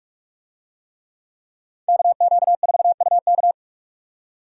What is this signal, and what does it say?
Morse code, heard as a keyed tone, about 49 wpm, 700 Hz. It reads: KQ4UK